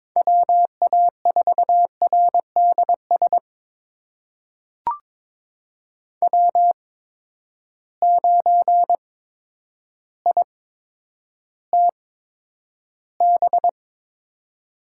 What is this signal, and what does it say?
Morse code, heard as a keyed tone, about 22 wpm, spaced as for 10 wpm, 700 Hz. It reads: WA4RDS W9ITB
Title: USA Call Signs - 8x Character Spacing - 22wpm